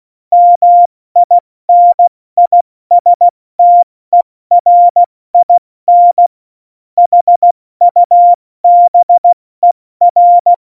Morse code, heard as a keyed tone, about 16 wpm, 700 Hz. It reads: MINISTERIN HUBER
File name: Morsekurs / Witze / Witz 14 Juli 2020, 70 F80